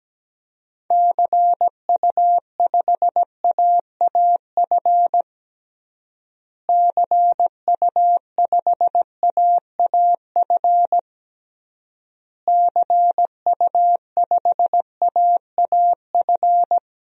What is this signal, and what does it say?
Morse code, heard as a keyed tone, about 17 wpm, 700 Hz. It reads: CU5AAF CU5AAF CU5AAF
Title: DXCC-100 Call Signs - 6 Characters - Repeated 3x - 17wpm